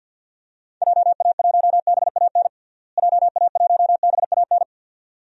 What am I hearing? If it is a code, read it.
Morse code, heard as a keyed tone, about 50 wpm, 700 Hz. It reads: JA1BAN JA1BAN
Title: DXCC-100 Call Signs - 6 Characters - Repeated 2x - 50wpm